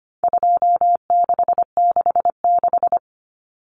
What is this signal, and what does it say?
Morse code, heard as a keyed tone, about 25 wpm, 700 Hz. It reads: 2666
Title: Common Cut Numbers - 4-Digit 25wpm